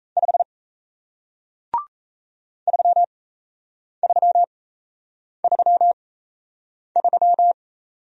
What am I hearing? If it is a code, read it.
Morse code, heard as a keyed tone, about 38 wpm, spaced as for 8 wpm, 700 Hz. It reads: 5 3333